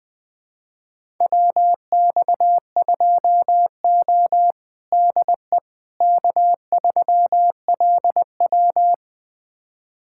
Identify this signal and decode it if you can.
Morse code, heard as a keyed tone, about 20 wpm, 700 Hz. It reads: WX2O DE K3LW
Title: USA Callsign DE Callsign - 4 Characters - 20wpm